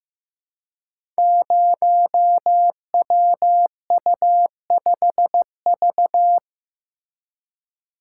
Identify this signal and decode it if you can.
Morse code, heard as a keyed tone, about 15 wpm, 700 Hz. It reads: 0WU5V
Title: Code Groups - 5 Letter:Numbers - 15wpm